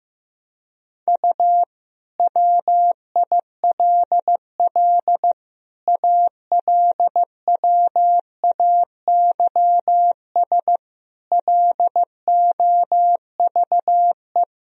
Morse code, heard as a keyed tone, about 16 wpm, 700 Hz. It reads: U WILL ALWAYS LOVE
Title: Sentences from Top 400 Words and Abbreviations 15wpm